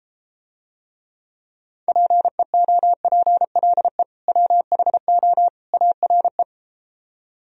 Morse code, heard as a keyed tone, about 33 wpm, 700 Hz. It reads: PEOPLE WHO ARE